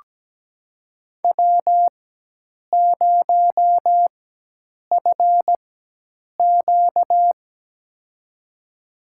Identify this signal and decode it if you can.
Morse code, heard as a keyed tone, about 17 wpm, spaced as for 8 wpm, 700 Hz. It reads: W0FQ
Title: USA Call Signs - 4x Character Spacing - 17wpm